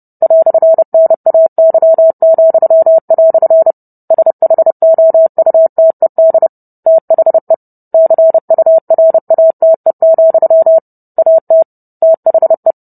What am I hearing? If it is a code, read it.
Morse code, heard as a keyed tone, about 30 wpm, 650 Hz. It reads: "NAY," SHOUTED THE CURATE, AT THE